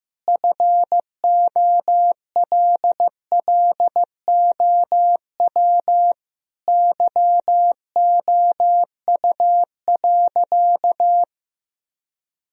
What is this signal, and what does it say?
Morse code, heard as a keyed tone, about 15 wpm, 700 Hz. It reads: FOLLOW YOU.
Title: Sentences from Top 1000 Words - Review Part 1 - 15wpm